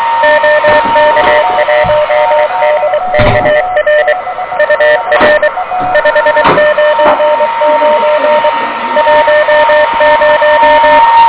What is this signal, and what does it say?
Morse code, heard as a keyed tone, about 23 wpm, 600 Hz, with noise. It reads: OK1DKR UR 599 10
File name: OK QRP 2023 / OK QRP 2023